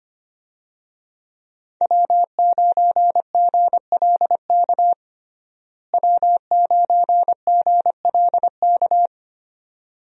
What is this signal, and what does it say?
Morse code, heard as a keyed tone, about 25 wpm, 700 Hz. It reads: W9GLK W9GLK
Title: USA Call Signs - 5 Characters - Repeated 2x - 25wpm